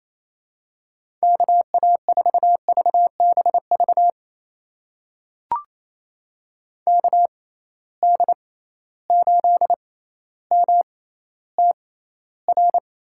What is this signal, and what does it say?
Morse code, heard as a keyed tone, about 27 wpm, spaced as for 14 wpm, 700 Hz. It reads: KA4VBV KD8MTR
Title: USA Call Signs - 6 Characters - 6x Character Spacing - 28wpm